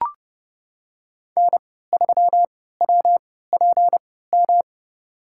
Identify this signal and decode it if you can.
Morse code, heard as a keyed tone, about 30 wpm, spaced as for 17 wpm, 700 Hz. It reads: N3WPM
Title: USA Call Signs - 5 Characters - 3x Character Spacing - 30wpm